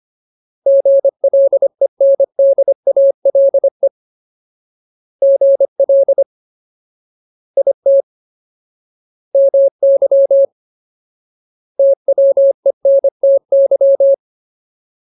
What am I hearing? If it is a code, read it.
Morse code, heard as a keyed tone, about 25 wpm, 550 Hz. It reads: GLENDALE GL IT MY TWENTY
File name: Ultimate Word List / 25WPM Ultimate